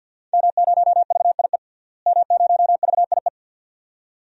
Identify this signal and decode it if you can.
Morse code, heard as a keyed tone, about 46 wpm, 700 Hz. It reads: M0VIE M0VIE
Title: DXCC-100 Call Signs - 5 Characters - Repeated 2x - 50wpm